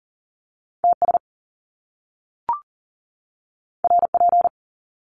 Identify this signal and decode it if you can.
Morse code, heard as a keyed tone, about 39 wpm, 700 Hz. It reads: TS RP
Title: Top Two Letter Combinations in Words 40wpm